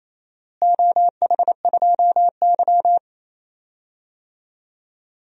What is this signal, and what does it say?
Morse code, heard as a keyed tone, about 27 wpm, 700 Hz. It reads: OH2Y
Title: DXCC-100 Call Signs - 4 Characters - Repeated 2x - 28wpm